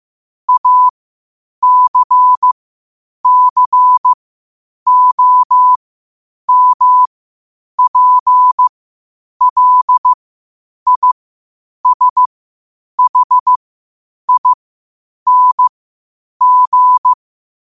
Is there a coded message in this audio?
Yes: Morse code from 0.5 to 17.1 s, heard as a keyed tone, about 15 wpm, spaced as for 8 wpm, 1000 Hz. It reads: ACCOMPLISHING